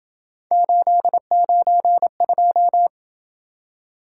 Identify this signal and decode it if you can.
Morse code, heard as a keyed tone, about 27 wpm, 700 Hz. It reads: 892